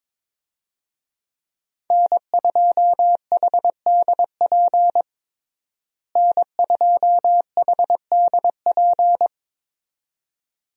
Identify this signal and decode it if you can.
Morse code, heard as a keyed tone, about 22 wpm, 700 Hz. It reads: N2HDP N2HDP
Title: USA Call Signs - 5 Characters - Repeated 2x - 22wpm